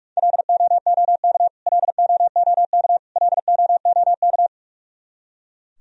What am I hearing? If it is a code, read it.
Morse code, heard as a keyed tone, about 44 wpm, 700 Hz. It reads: LOOK LOOK LOOK
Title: Top 100 Words Repeated 45wpm